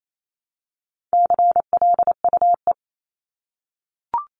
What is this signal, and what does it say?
Morse code, heard as a keyed tone, about 28 wpm, 700 Hz. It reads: CLUE
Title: Top 1751-2000 Words 28wpm